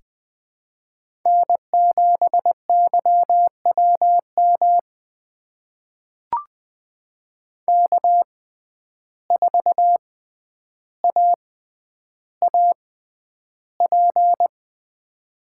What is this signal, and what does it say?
Morse code, heard as a keyed tone, about 20 wpm, spaced as for 10 wpm, 700 Hz. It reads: N7YWM K4AAP
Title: USA Call Signs - 5 Characters - 6x Character Spacing - 20wpm